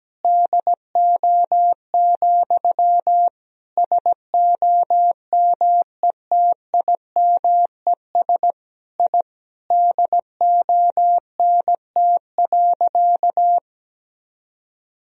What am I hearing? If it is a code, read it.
Morse code, heard as a keyed tone, about 17 wpm, 700 Hz. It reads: DO, SOMETIMES I DONT.
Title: Sentences from Top 1000 Words - Review Part 3 - 17wpm